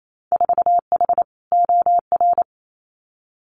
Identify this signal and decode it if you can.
Morse code, heard as a keyed tone, about 28 wpm, 700 Hz. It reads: 4H OR
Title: Winter Field Day Exchanges Encore 28wpm